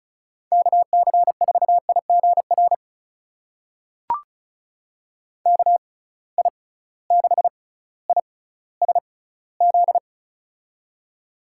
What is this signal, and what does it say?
Morse code, heard as a keyed tone, about 34 wpm, spaced as for 18 wpm, 700 Hz. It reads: KC4IGR KI6ISZ
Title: USA Call Signs - 6 Characters - 6x Character Spacing - 35wpm